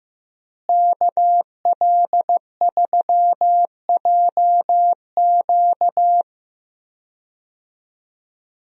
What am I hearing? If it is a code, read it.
Morse code, heard as a keyed tone, about 15 wpm, 700 Hz. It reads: KL3JQ
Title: USA Call Signs - 5 Characters - Repeated 2x - 15wpm